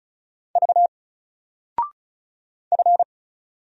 Morse code, heard as a keyed tone, about 33 wpm, 700 Hz. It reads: V F
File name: Single-Letter-Number-35wpm